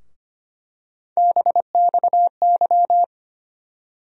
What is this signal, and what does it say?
Morse code, heard as a keyed tone, about 25 wpm, 700 Hz. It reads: BXY
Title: USA Call Sign Postfixes - 3 Letters 25wpm